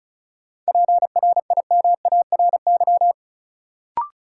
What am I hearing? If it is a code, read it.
Morse code, heard as a keyed tone, about 36 wpm, 700 Hz. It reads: PRIMARY
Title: Sentences from Top 1000 Words - Review Part 5 - New Word Drill - 35wpm